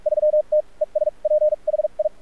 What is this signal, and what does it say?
Morse code, heard as a keyed tone, about 44 wpm, spaced as for 33 wpm, 610 Hz. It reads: 3 TESPHI
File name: archived / CRSnet